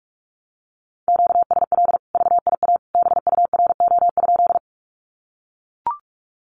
Morse code, heard as a keyed tone, about 44 wpm, 700 Hz. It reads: QSL VIA BURO?
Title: Common QSO Phrases 45wpm